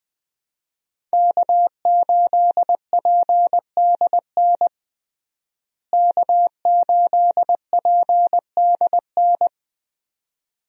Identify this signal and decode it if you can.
Morse code, heard as a keyed tone, about 21 wpm, 700 Hz. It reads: K8PDN K8PDN
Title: USA Call Signs - 5 Characters - Repeated 2x - 20wpm